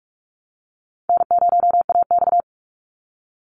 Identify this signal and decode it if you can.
Morse code, heard as a keyed tone, about 44 wpm, 700 Hz. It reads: N0AX
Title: Call Signs Repeated 3x - 45wpm